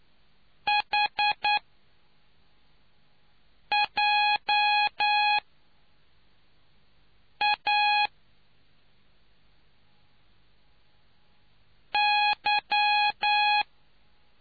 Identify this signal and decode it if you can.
Morse code, heard as a keyed tone, about 9 wpm, 830 Hz. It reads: H J A Y